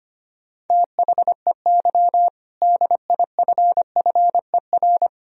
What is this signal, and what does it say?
Morse code, heard as a keyed tone, about 25 wpm, 700 Hz. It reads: THEY DIFFER
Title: Sentences from Top 1000 Words - Review Part 2 - 25wpm